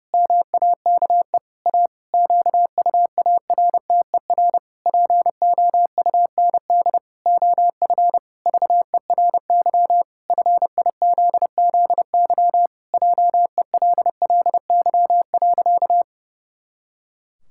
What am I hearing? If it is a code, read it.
Morse code, heard as a keyed tone, about 32 wpm, 700 Hz. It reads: MAKE A QUARTER POUND OF VERY FIZZY JELLY.
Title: Pangram Challenge v2 - 30wpm